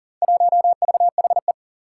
Morse code, heard as a keyed tone, about 40 wpm, 700 Hz. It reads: 1VHE
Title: Four Challenging Letter-Number Combinations 40wpm